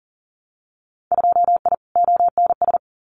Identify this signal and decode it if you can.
Morse code, heard as a keyed tone, about 40 wpm, 700 Hz. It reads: 2I ONS
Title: Winter Field Day Exchanges Encore 40wpm